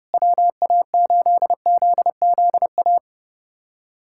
Morse code, heard as a keyed tone, about 30 wpm, 700 Hz. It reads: WA8ZZA
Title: USA Callsign DE Callsign - 6 Characters - 30wpm